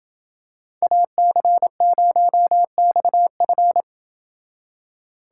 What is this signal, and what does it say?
Morse code, heard as a keyed tone, about 28 wpm, 700 Hz. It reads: AC0XF